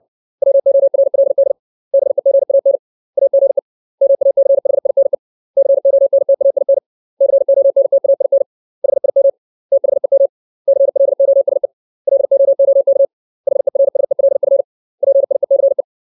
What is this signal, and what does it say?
Morse code, heard as a keyed tone, about 60 wpm, 550 Hz. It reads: WORLD BEGAN AGE MACHINE CONTAIN CONTAIN HIM THEM CLOSE LOOK HERSELF PIECE
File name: 632 words / 632 Words 60WPM